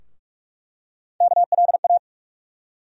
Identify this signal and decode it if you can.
Morse code, heard as a keyed tone, about 46 wpm, 700 Hz. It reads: KLA
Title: USA Call Sign Postfixes - 3 Letters 45wpm